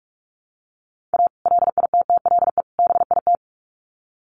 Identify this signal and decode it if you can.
Morse code, heard as a keyed tone, about 44 wpm, 700 Hz. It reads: A LITTLE BIT
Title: Sets of 3 Words 45wpm